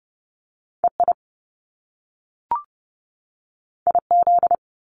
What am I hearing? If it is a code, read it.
Morse code, heard as a keyed tone, about 30 wpm, 700 Hz. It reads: E I I Z
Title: Top Two Letter Combinations in Words 30wpm